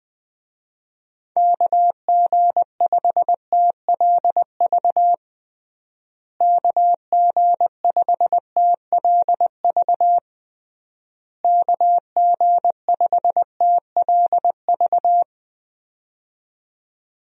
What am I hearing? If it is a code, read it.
Morse code, heard as a keyed tone, about 20 wpm, 700 Hz. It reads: KG5TLV KG5TLV KG5TLV
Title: USA Call Signs - 6 Characters - Repeated 3x - 20wpm